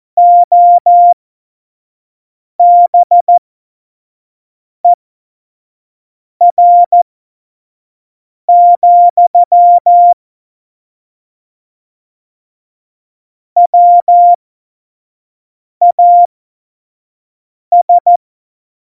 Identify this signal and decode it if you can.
Morse code, heard as a keyed tone, about 14 wpm, spaced as for 5 wpm, 700 Hz. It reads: OBER, WAS
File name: Morsekurs / Witze / Witz 26. November 2019, 25 F70